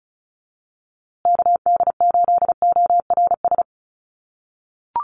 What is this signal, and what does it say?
Morse code, heard as a keyed tone, about 34 wpm, 700 Hz. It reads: KD8ORS
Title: USA Call Signs Repeated 2x 35wpm